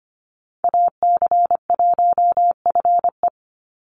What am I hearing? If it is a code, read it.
Morse code, heard as a keyed tone, about 25 wpm, 700 Hz. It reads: AC1FE
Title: Call Signs - 25wpm